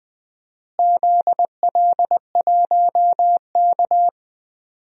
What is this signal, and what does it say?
Morse code, heard as a keyed tone, about 21 wpm, 700 Hz. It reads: ZL1K
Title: DXCC-100 Call Signs - 4 Characters - Repeated 2x - 20wpm